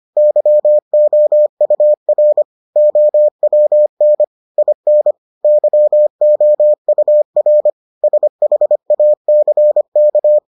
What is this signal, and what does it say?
Morse code, heard as a keyed tone, about 25 wpm, 600 Hz. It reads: YOUR OWN IN YOUR SHACK